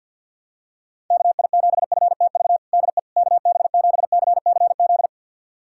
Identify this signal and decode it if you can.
Morse code, heard as a keyed tone, about 52 wpm, 700 Hz. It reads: KI7FTV DE KB7CCZ
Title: USA Callsign DE Callsign - 6 Characters - 50wpm